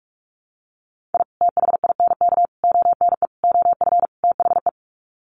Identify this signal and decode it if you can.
Morse code, heard as a keyed tone, about 44 wpm, 700 Hz. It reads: I THINK ONE OF THE
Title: Sets of 5 Words 45wpm